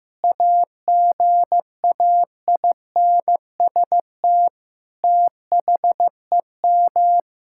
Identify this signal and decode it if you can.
Morse code, heard as a keyed tone, about 15 wpm, 700 Hz. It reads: AGAINST THEM
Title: Sentences from Top 1000 Words - Review Part 6 - 15wpm